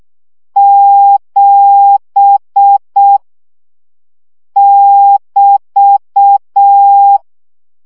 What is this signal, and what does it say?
Morse code, heard as a keyed tone, about 6 wpm, 800 Hz. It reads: 7 <BT>
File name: CW gyakorlas 7 5 baud